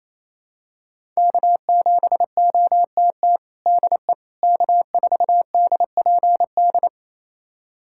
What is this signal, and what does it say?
Morse code, heard as a keyed tone, about 28 wpm, 700 Hz. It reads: K7OTT DE K4DPD
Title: USA Callsign DE Callsign - 5 Characters - 28wpm